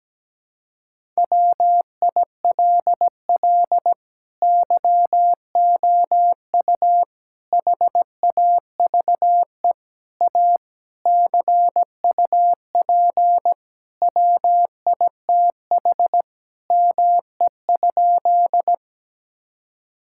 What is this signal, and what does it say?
Morse code, heard as a keyed tone, about 17 wpm, 700 Hz. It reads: WILL YOU HAVE A CUP WITH ME?
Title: Sentences from Top 1600 Words 17wpm